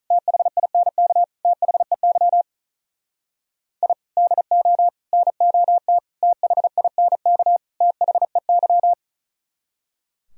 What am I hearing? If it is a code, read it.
Morse code, heard as a keyed tone, about 46 wpm, 700 Hz. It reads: THINK THEY I DO NOT THINK THEY